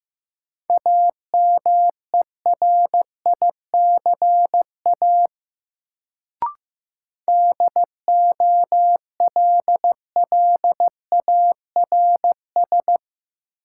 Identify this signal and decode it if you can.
Morse code, heard as a keyed tone, about 15 wpm, 700 Hz. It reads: AMERICA DOLLARS
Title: Sets of 5 Words - New Word Drill - 15wpm